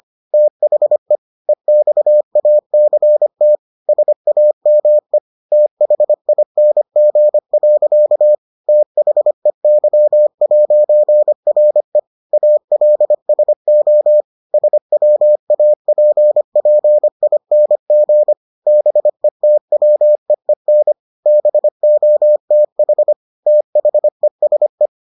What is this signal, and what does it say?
Morse code, heard as a keyed tone, about 25 wpm, 600 Hz. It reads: THE EXACT SAME THING. THEY'RE ALSO SWAPPING BETWEEN BOTH THESE